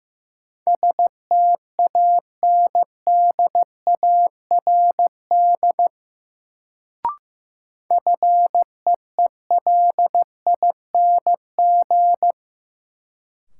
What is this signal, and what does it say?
Morse code, heard as a keyed tone, about 15 wpm, 700 Hz. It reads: STANDARD FEELING
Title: Sentences from Top 1200 Words - New Word Drill Alt - 15wpm